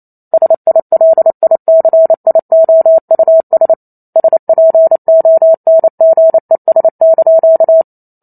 Morse code, heard as a keyed tone, about 28 wpm, 650 Hz. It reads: SILICIOUS SPONGES)